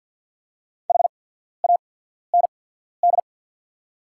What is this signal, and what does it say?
Morse code, heard as a keyed tone, about 48 wpm, spaced as for 14 wpm, 700 Hz. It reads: HAND
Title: 8x Character Spacing - Top 200 Words - 50wpm